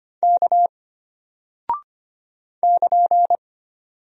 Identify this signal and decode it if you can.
Morse code, heard as a keyed tone, about 25 wpm, 700 Hz. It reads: K <KN>
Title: Single Character (Letter:Number:Punctuation:Prosign) Alt 25wpm